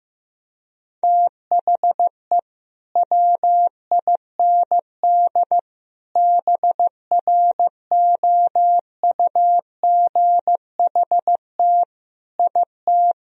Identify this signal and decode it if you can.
Morse code, heard as a keyed tone, about 15 wpm, 700 Hz. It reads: THE WIND BROUGHT IT
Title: Sentences from Top 1000 Words - Review Part 2 - 15wpm